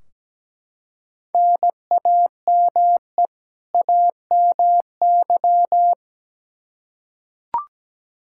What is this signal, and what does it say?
Morse code, heard as a keyed tone, about 17 wpm, 700 Hz. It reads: NAME AMY
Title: Practice QSO - Exchange 1 - Beginning - 17wpm